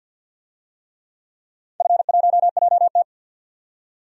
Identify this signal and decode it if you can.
Morse code, heard as a keyed tone, about 52 wpm, 700 Hz. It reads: F1JT
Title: DXCC-100 Call Signs - 4 Characters - 50wpm